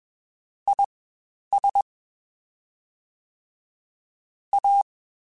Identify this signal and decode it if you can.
Morse code, heard as a keyed tone, about 21 wpm, spaced as for 10 wpm, 800 Hz. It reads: IS A
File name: Spaced Characters / Spaced Characters-mp3s / Weather 4 14 sp @20